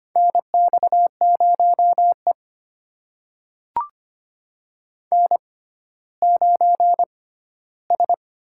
Morse code, heard as a keyed tone, about 25 wpm, spaced as for 14 wpm, 700 Hz. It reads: NX0E N9S
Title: USA Call Signs - 6x Character Spacing - 25wpm